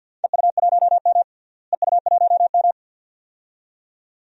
Morse code, heard as a keyed tone, about 49 wpm, 700 Hz. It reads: EU1M EU1M
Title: DXCC-100 Call Signs - 4 Characters - Repeated 2x - 50wpm